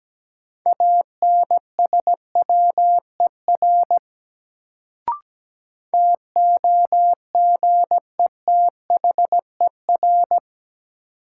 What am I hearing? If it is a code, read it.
Morse code, heard as a keyed tone, about 17 wpm, 700 Hz. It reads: ANSWER TOGETHER
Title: Sentences from Top 300 Words - New Word Drill - 17wpm